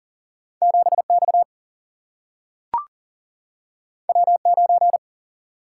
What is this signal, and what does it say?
Morse code, heard as a keyed tone, about 40 wpm, 700 Hz. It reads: 7 X W 9
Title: Single-Letter Single-Number 40wpm